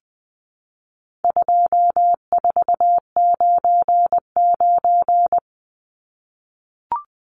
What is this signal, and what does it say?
Morse code, heard as a keyed tone, about 20 wpm, 700 Hz. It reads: 2499
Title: Numbers - 4-Digit 20wpm